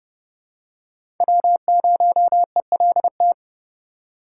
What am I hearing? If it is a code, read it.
Morse code, heard as a keyed tone, about 30 wpm, 700 Hz. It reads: W0ELT
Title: USA Call Signs - 5 Characters - 3x Character Spacing - 30wpm